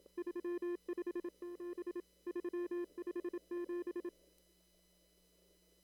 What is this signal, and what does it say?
Morse code, heard as a keyed tone, about 27 wpm, 350 Hz. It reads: 357 357
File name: CW / kt5j wpx cw-op-k5pi